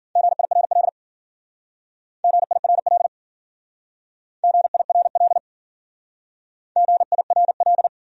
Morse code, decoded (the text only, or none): GIRL GIRL GIRL GIRL